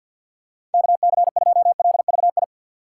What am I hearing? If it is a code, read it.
Morse code, heard as a keyed tone, about 53 wpm, 700 Hz. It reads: KC2LVI
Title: USA Call Signs - 6 Characters - 3x Character Spacing - 50wpm